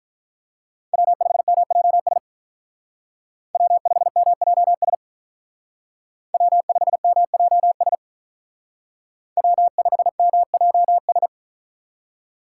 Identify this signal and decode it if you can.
Morse code, heard as a keyed tone, about 43 wpm, 700 Hz. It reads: W5MJS W5MJS W5MJS W5MJS
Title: Speed Racing - USA 1x3 Call Signs 35wpm